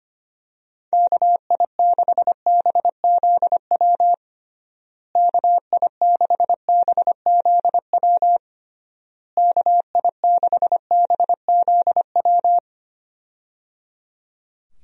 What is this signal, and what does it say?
Morse code, heard as a keyed tone, about 25 wpm, 700 Hz. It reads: KI6BZW KI6BZW KI6BZW
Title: USA Call Signs - 6 Characters - Repeated 3x - 25wpm